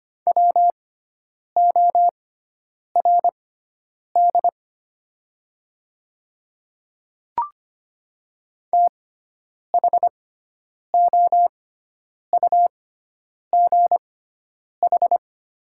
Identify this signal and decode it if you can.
Morse code, heard as a keyed tone, about 25 wpm, spaced as for 9 wpm, 690 Hz. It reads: WORD THOUGH
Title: 6x Character Spacing - Top 500 Words - 25wpm